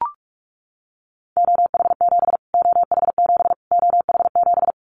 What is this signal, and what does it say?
Morse code, heard as a keyed tone, about 44 wpm, 700 Hz. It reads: OH7 OH7 OH7
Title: Callsign Prefixes for Top 25 DXCC - Repeated 3x - 45wpm